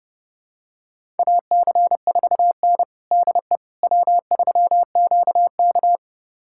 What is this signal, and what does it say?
Morse code, heard as a keyed tone, about 31 wpm, 700 Hz. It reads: AC4N DE W3QK
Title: USA Callsign DE Callsign - 4 Characters - 30wpm